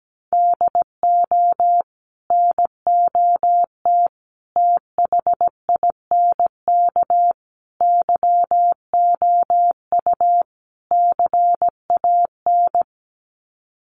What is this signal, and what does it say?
Morse code, heard as a keyed tone, about 17 wpm, 700 Hz. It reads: DO NOT THINK YOU CAN
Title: Sets of 5 Words 17wpm